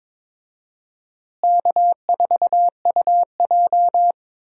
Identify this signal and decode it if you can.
Morse code, heard as a keyed tone, about 22 wpm, 700 Hz. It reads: K4UJ
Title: USA Call Signs - 4 Characters - Repeated 3x - 22wpm